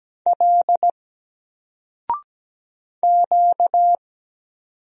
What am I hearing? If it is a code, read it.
Morse code, heard as a keyed tone, about 17 wpm, 700 Hz. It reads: L Q
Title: Single Character (Letter:Number:Punctuation:Prosign) 17wpm